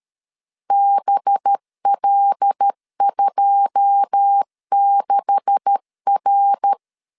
Morse code, heard as a keyed tone, about 13 wpm, 800 Hz. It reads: BL26R
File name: CW / DET / 1 9 1500C S C 12wpm 3811